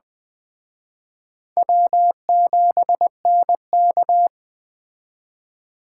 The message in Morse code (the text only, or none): W7NK